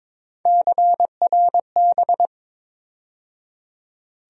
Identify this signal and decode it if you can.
Morse code, heard as a keyed tone, about 22 wpm, 700 Hz. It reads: CRB
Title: USA Call Sign Postfixes - 3 Letters 22wpm